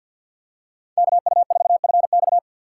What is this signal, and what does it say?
Morse code, heard as a keyed tone, about 51 wpm, 700 Hz. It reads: KU4FX
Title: USA Call Signs - 5 Characters - 2x Character Spacing - 50wpm